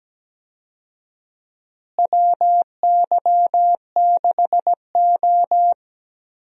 Morse code, heard as a keyed tone, about 17 wpm, 700 Hz. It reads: WY6O
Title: USA Call Signs - 4 Characters - Repeated 2x - 17wpm